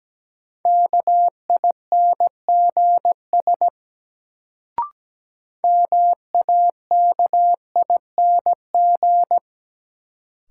Morse code, heard as a keyed tone, about 17 wpm, 700 Hz. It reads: KINGS MAKING
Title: Sentences from Top 1000 Words - Review Part 6 - New Word Drill - 17wpm